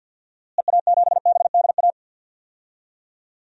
Morse code, heard as a keyed tone, about 47 wpm, 700 Hz. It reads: EA7BDA
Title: DXCC-100 Call Signs - 6 Characters - Repeated 2x - 50wpm